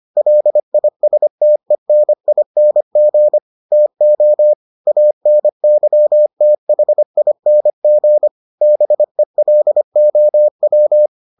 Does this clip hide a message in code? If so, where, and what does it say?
Yes: Morse code from 0.2 to 11.1 s, heard as a keyed tone, about 25 wpm, 600 Hz. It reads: LISTENING TO ANYTHING BELOW